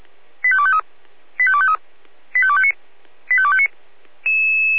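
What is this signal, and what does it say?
Five-tone selective call (ZVEI), heard as quick runs of five tones, tones of about 70 ms. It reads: 85253 85253 85259 85259
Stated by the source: ZVEI five-tone selective call received on SDR.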